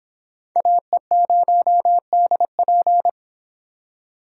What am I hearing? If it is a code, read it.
Morse code, heard as a keyed tone, about 26 wpm, 700 Hz. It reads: AE0DP